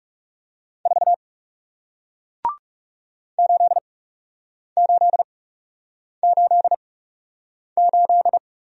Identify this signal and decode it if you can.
Morse code, heard as a keyed tone, about 40 wpm, spaced as for 8 wpm, 700 Hz. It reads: 4 8888